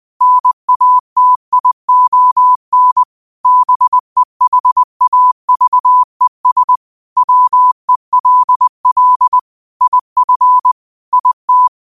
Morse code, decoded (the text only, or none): NATION BEHAVES WELL IF IT